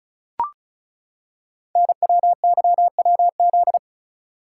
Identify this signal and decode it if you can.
Morse code, heard as a keyed tone, about 36 wpm, 700 Hz. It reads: NWYWZ
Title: Code Groups - 5 Letters - 35wpm